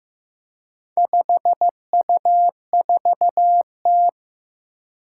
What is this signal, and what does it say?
Morse code, heard as a keyed tone, about 15 wpm, 700 Hz. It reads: 5U4T
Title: Four Challenging Letter-Number Combinations 15wpm